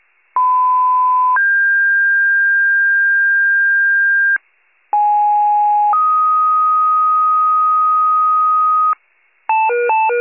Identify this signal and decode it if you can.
Two-tone page, heard as a one-second tone, then a three-second tone, about 1010 then 1600 Hz, about 830 then 1250 Hz, with noise. It is heard from 0.4 to 8.9 s.
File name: wcalert